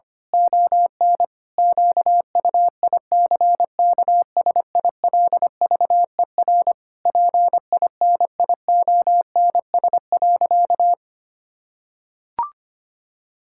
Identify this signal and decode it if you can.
Morse code, heard as a keyed tone, about 25 wpm, 700 Hz. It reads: ON QUICKSILVER PINIONS.
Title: Pangram Challenge v2 - 25wpm